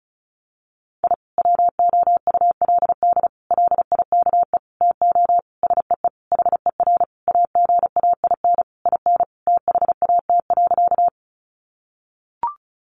Morse code, heard as a keyed tone, about 34 wpm, 700 Hz. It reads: I WOULD LIKE TO SEE HER AGAIN IN THAT.
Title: Sentences from Top 200 Words 35wpm